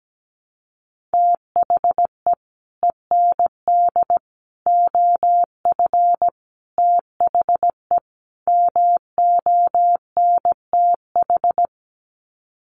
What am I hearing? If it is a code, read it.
Morse code, heard as a keyed tone, about 17 wpm, 700 Hz. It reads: THE END OF THE MONTH
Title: Sets of 5 Words Encore 17wpm